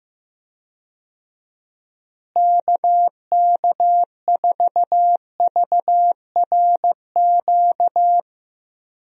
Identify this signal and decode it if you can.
Morse code, heard as a keyed tone, about 16 wpm, 700 Hz. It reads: KK4VRQ
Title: USA Call Signs - 6 Characters - Repeated 3x - 15wpm